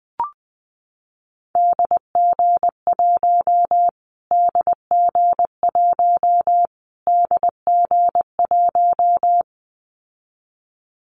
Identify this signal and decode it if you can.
Morse code, heard as a keyed tone, about 20 wpm, 700 Hz. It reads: DG1 DG1 DG1
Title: Callsign Prefixes for Top 25 DXCC - Repeated 3x - 20wpm